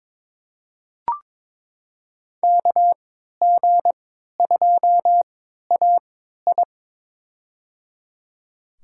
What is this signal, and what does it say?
Morse code, heard as a keyed tone, about 21 wpm, spaced as for 12 wpm, 700 Hz. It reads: KG2AI
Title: USA Call Signs - 5 Characters - 3x Character Spacing - 22wpm